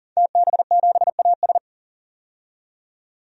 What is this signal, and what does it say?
Morse code, heard as a keyed tone, about 40 wpm, 700 Hz. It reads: TB7AS
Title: DXCC-100 Call Signs - 5 Characters - 40wpm